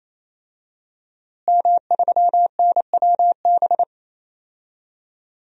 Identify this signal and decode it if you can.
Morse code, heard as a keyed tone, about 27 wpm, 700 Hz. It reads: M3NWB
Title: DXCC-100 Call Signs - 5 Characters - Repeated 3x - 28wpm